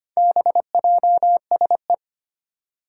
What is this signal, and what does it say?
Morse code, heard as a keyed tone, about 25 wpm, 700 Hz. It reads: BJSE
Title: Four Challenging Letter-Number Combinations 25wpm